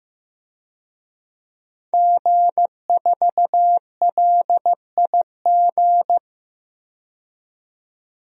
Morse code, heard as a keyed tone, about 15 wpm, 700 Hz. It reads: G4LIG
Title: DXCC-100 Call Signs - 5 Characters - 15wpm